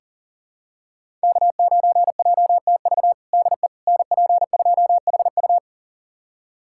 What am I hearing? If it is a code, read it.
Morse code, heard as a keyed tone, about 40 wpm, 700 Hz. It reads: K9JTV DE NP2HU
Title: USA Callsign DE Callsign - 5 Characters - 40wpm